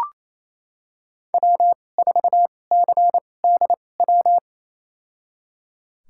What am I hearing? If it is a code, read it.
Morse code, heard as a keyed tone, about 28 wpm, spaced as for 20 wpm, 700 Hz. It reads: W4CDW
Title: USA Call Signs - 5 Characters - 2x Character Spacing - 28wpm